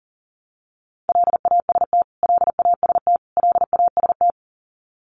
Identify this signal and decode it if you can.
Morse code, heard as a keyed tone, about 40 wpm, 700 Hz. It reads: LAST LAST LAST
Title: Top 500 Words Repeated 40wpm